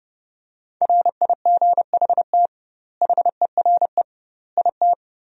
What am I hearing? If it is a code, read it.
Morse code, heard as a keyed tone, about 30 wpm, 700 Hz. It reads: RIGHT HERE IT
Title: 2x Word-Spacing - Sentences from Top 300 Words - 30wpm